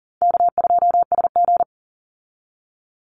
Morse code, heard as a keyed tone, about 37 wpm, 700 Hz. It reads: K2SG
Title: Easy Call Signs Repeated 3x - 40wpm